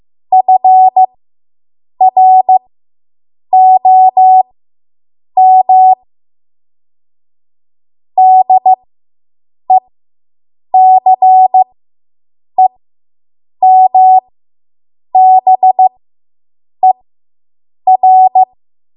Morse code, heard as a keyed tone, about 15 wpm, spaced as for 7 wpm, 750 Hz. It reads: FROM DECEMBER